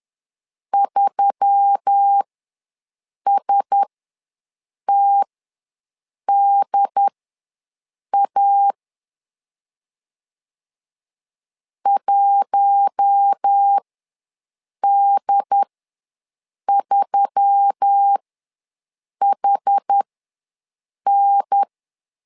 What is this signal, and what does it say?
Morse code, heard as a keyed tone, about 10 wpm, spaced as for 6 wpm, 800 Hz. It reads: 3STDA 1D3HN
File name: LEZIONI / 4 Q1S3 ws 10wpm 2437